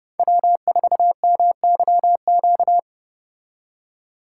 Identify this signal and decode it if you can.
Morse code, heard as a keyed tone, about 29 wpm, 700 Hz. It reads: W4MYQ